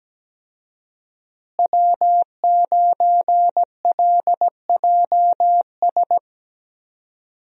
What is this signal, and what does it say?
Morse code, heard as a keyed tone, about 17 wpm, 700 Hz. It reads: W9LJS
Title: USA Call Signs - 5 Characters - Repeated 3x - 17wpm